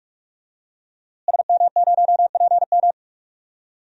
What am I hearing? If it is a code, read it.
Morse code, heard as a keyed tone, about 42 wpm, 700 Hz. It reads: SM0PM
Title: DXCC-100 Call Signs - 5 Characters - Repeated 2x - 45wpm